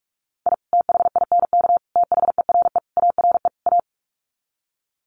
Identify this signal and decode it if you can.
Morse code, heard as a keyed tone, about 44 wpm, 700 Hz. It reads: I THINK THERE ARE A
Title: Sets of 5 Words Encore2 45wpm